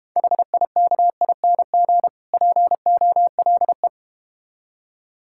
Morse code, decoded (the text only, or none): HIKING POLE